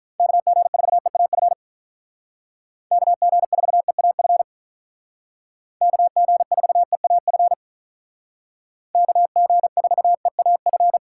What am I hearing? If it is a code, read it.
Morse code, heard as a keyed tone, about 46 wpm, 700 Hz. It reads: KG4EAF KG4EAF KG4EAF KG4EAF